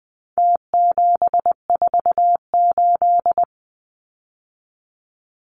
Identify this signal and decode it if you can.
Morse code, heard as a keyed tone, about 20 wpm, 700 Hz. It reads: T748
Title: Numbers - Contest Style - 4-Digit 20wpm